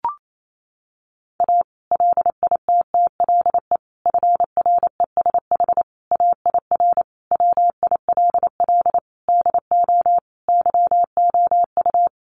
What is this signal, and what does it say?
Morse code, heard as a keyed tone, about 28 wpm, 700 Hz. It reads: A LITTLE FRESH AIR WILL DO YOU
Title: Sentences from Top 1200 Words 28wpm